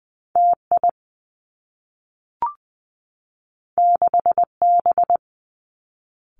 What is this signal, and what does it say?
Morse code, heard as a keyed tone, about 20 wpm, 700 Hz. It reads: T I 6 B
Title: Two Challenging Letter-Number Combinations 20wpm